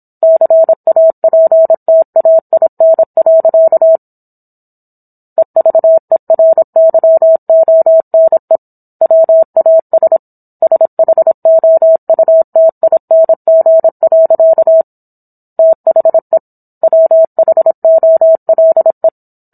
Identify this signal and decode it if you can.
Morse code, heard as a keyed tone, about 27 wpm, 650 Hz. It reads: CAPTAIN. EVERYONE WAS SHOUTING. THE WHOLE